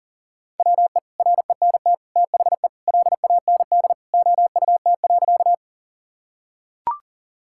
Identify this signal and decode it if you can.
Morse code, heard as a keyed tone, about 40 wpm, 700 Hz. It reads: WE RENT THE LAND OUT.
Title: Sentences from Top 1000 Words - Review Part 5 - 40wpm